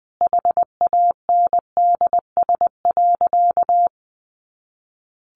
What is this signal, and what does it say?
Morse code, heard as a keyed tone, about 20 wpm, 700 Hz. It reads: HANDS.
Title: Easy Phrases 20wpm